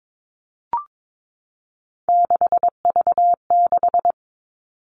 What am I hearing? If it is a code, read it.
Morse code, heard as a keyed tone, about 22 wpm, 700 Hz. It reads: 6V6
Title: All Cut Numbers - 3-Digit 22wpm